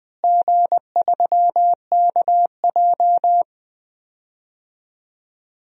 Morse code, heard as a keyed tone, about 21 wpm, 700 Hz. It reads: G3KJ
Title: DXCC-100 Call Signs - 4 Characters - Repeated 2x - 20wpm